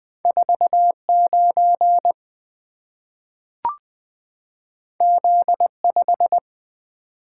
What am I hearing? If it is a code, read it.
Morse code, heard as a keyed tone, about 20 wpm, 700 Hz. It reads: 49 Z5
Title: Two Letter-Number Combinations - 20wpm